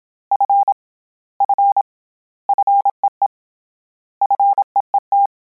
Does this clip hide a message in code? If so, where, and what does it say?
Yes: Morse code from 0.3 to 5.3 s, heard as a keyed tone, about 27 wpm, 800 Hz. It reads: F F FEE FEET